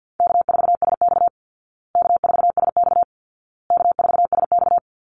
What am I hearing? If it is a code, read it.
Morse code, heard as a keyed tone, about 50 wpm, 700 Hz. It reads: K4SX K4SX K4SX
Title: Easy Call Signs Repeated 3x - 50wpm